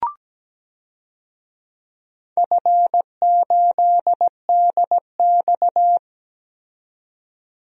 Morse code, heard as a keyed tone, about 17 wpm, 700 Hz. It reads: F8DX
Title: DXCC-100 Call Signs - 4 Characters - 17wpm